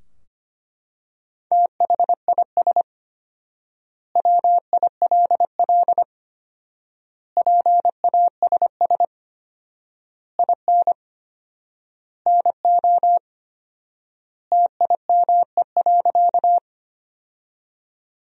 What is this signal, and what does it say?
Morse code, heard as a keyed tone, about 25 wpm, 700 Hz. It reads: THIS WILL PASS IN NO TIME.
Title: 4x Word-Spacing - Sentences from Top 400 Words - 25wpm